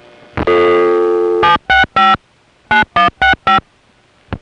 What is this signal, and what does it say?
DTMF, heard as telephone touch tones, tones of about 130 ms. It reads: 765 9165
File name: phonetrips / SoCal1HQ